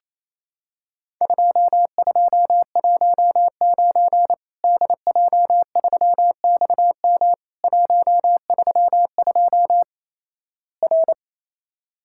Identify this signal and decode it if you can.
Morse code, heard as a keyed tone, about 28 wpm, 700 Hz. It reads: 2219 DJ3XM 132 R
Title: Worked All Europe DX Contest - QTC Exchange - 28wpm